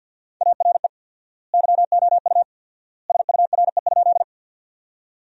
Morse code, heard as a keyed tone, about 47 wpm, 700 Hz. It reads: ARE YOU SURE?
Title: 4x Word-Spacing - Sentences from Top 300 Words - 50wpm